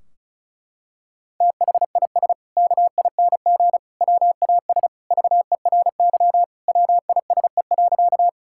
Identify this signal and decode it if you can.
Morse code, heard as a keyed tone, about 36 wpm, 700 Hz. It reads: THIS KING WAS VERY WISE.
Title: Sentences from Top 1600 Words 35wpm